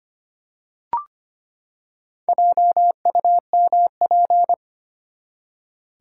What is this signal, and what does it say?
Morse code, heard as a keyed tone, about 25 wpm, 700 Hz. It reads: JUMP
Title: Four Letter Words - 25wpm